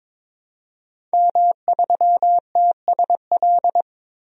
Morse code, heard as a keyed tone, about 22 wpm, 700 Hz. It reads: M3TSL
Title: DXCC-100 Call Signs - 5 Characters - 22wpm